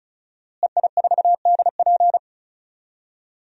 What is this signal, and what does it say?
Morse code, heard as a keyed tone, about 36 wpm, 700 Hz. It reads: EI4DP
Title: DXCC-100 Call Signs - 5 Characters - Repeated 2x - 35wpm